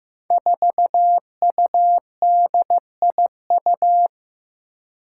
Morse code, heard as a keyed tone, about 15 wpm, 700 Hz. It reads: 4UDIU
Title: Five Challenging Letter-Number Combinations 15wpm